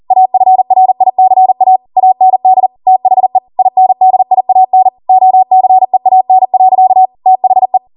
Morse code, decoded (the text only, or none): AFRICA AND THE INDIAN OCEAN. THE